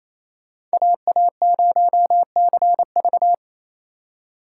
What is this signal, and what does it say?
Morse code, heard as a keyed tone, about 36 wpm, 700 Hz. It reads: AA0CV